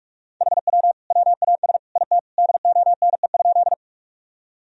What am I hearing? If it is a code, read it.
Morse code, heard as a keyed tone, about 46 wpm, 700 Hz. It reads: HW WAS IT DONE?
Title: Sentences from Top 500 Words and Abbreviations 45wpm